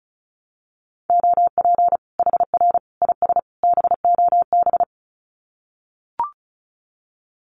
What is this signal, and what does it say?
Morse code, heard as a keyed tone, about 34 wpm, 700 Hz. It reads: OP HR IS BOB
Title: Common QSO Phrases 35wpm